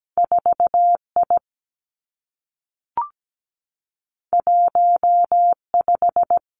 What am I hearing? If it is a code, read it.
Morse code, heard as a keyed tone, about 17 wpm, 700 Hz. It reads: 4I 15
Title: Two Challenging Letter-Number Combinations 17wpm